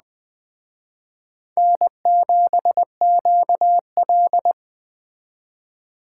Morse code, heard as a keyed tone, about 20 wpm, 700 Hz. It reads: N7QL
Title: Speed Racing - USA 1x2 Call Signs 17wpm